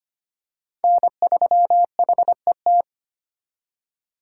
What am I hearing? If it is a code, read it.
Morse code, heard as a keyed tone, about 25 wpm, 700 Hz. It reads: N3HET
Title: USA Call Signs - 5 Characters - 2x Character Spacing - 25wpm